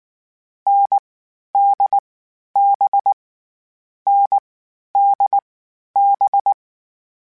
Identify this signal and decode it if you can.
Morse code, heard as a keyed tone, about 19 wpm, 800 Hz. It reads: N D B N D B
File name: Short Runs / Short Runs-mp3s / NDB NDB6 @18